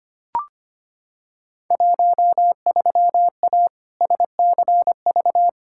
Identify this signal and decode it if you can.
Morse code, heard as a keyed tone, about 25 wpm, 700 Hz. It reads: 13A SCV
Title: Field Day Exchanges Encore - 25wpm